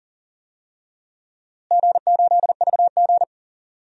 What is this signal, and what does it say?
Morse code, heard as a keyed tone, about 40 wpm, 700 Hz. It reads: G8VG
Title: DXCC-100 Call Signs - 4 Characters - Repeated 3x - 40wpm